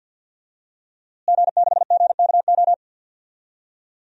Morse code, heard as a keyed tone, about 46 wpm, 700 Hz. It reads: G6GKO
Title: DXCC-100 Call Signs - 5 Characters - Repeated 3x - 50wpm